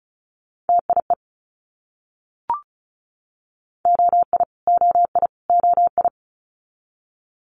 Morse code, heard as a keyed tone, about 36 wpm, 700 Hz. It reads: TIE OI OI OI
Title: Multi-letter Phonemes Repeated 3x 35wpm